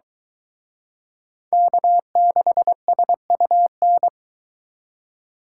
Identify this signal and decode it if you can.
Morse code, heard as a keyed tone, about 22 wpm, 700 Hz. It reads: K6SUN